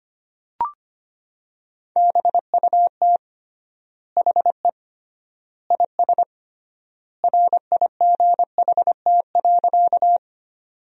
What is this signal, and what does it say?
Morse code, heard as a keyed tone, about 25 wpm, 700 Hz. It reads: BUT HE IS RIGHT.
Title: 3x Word-Spacing - Sentences from Top 200 Words - 25wpm